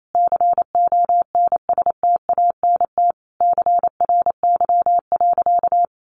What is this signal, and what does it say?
Morse code, heard as a keyed tone, about 28 wpm, 700 Hz. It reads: CONSTANT CRY.
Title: Sentences from Top 1500 Words 28wpm